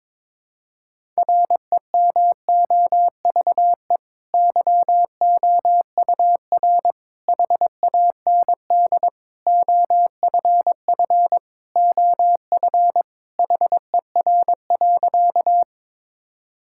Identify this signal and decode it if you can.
Morse code, heard as a keyed tone, about 22 wpm, 700 Hz. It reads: REMOVE YOUR HAND OFF OF HER.
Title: Sentences from Top 1000 Words - Review Part 5 - 22wpm